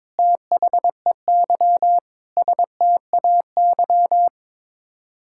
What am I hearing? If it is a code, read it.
Morse code, heard as a keyed tone, about 22 wpm, 700 Hz. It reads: THEY STAY
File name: practice / 22wpm / Sets of 8 Words - 22wpm